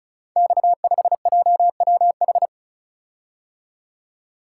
Morse code, heard as a keyed tone, about 36 wpm, 700 Hz. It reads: X5JWH
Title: Code Groups - 5 Letter:Numbers - 35wpm